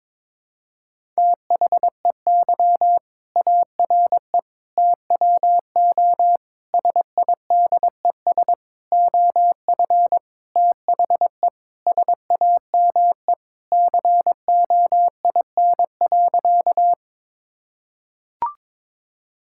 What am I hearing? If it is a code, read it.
Morse code, heard as a keyed tone, about 22 wpm, 700 Hz. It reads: THEY ARE TWO SIDES OF THE SAME COIN.
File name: Sentences with 7-Word Phrases - 22wpm